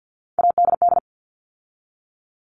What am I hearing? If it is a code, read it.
Morse code, heard as a keyed tone, about 50 wpm, 700 Hz. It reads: ADD
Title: All Cut Numbers - 3-Digit 50wpm